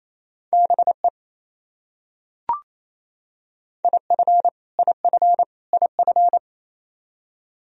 Morse code, heard as a keyed tone, about 28 wpm, 700 Hz. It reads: BE IF IF IF
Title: Top 100 Words Repeated 28wpm